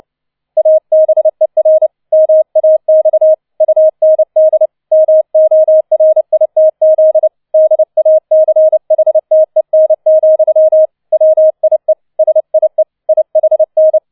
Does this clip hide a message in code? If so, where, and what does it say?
Yes: Morse code from 0.6 to 14.0 s, heard as a keyed tone, about 29 wpm, 630 Hz. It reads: ABER MAX UND MORITZ DACHTEN, WIE SIE IHN